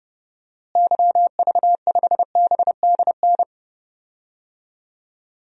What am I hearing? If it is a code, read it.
Morse code, heard as a keyed tone, about 30 wpm, 700 Hz. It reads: YV5BDN
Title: DXCC-100 Call Signs - 6 Characters - Repeated 2x - 30wpm